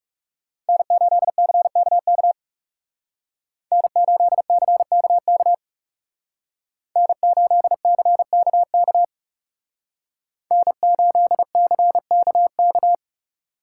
Morse code, heard as a keyed tone, about 36 wpm, 700 Hz. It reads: N8CKK N8CKK N8CKK N8CKK
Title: Speed Racing - USA Call Signs 30wpm